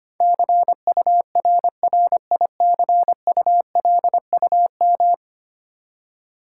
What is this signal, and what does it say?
Morse code, heard as a keyed tone, about 25 wpm, 700 Hz. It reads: CURRICULUM
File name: 10 Letter Words - First 3 Letters Given - 25wpm